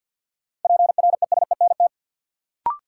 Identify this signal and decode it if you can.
Morse code, heard as a keyed tone, about 53 wpm, 700 Hz. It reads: PRESENT
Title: Sentences from Top 800 Words - New Word Drill - 50wpm